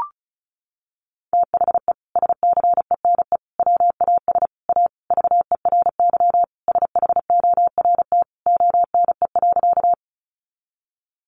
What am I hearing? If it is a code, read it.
Morse code, heard as a keyed tone, about 34 wpm, 700 Hz. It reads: THE SCENE WAS A VERY SHORT ONE.
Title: Sentences from Top 1100 Words 35wpm